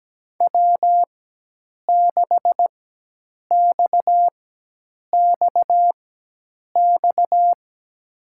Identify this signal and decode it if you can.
Morse code, heard as a keyed tone, about 17 wpm, spaced as for 8 wpm, 700 Hz. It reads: W6XXX
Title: USA Call Signs - 5 Characters - 4x Character Spacing - 17wpm